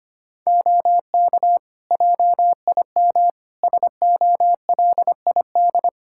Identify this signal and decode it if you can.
Morse code, heard as a keyed tone, about 25 wpm, 700 Hz. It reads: OK JIM SOLID
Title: Practice QSO - Exchange 3 - More Info - 25wpm